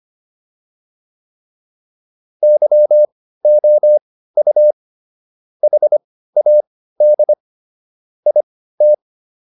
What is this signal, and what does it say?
Morse code, heard as a keyed tone, about 25 wpm, spaced as for 15 wpm, 600 Hz. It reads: YOU HAD IT